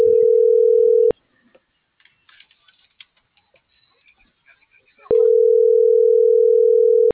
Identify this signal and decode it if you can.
Ringback tone (North America), 440 + 480 Hz.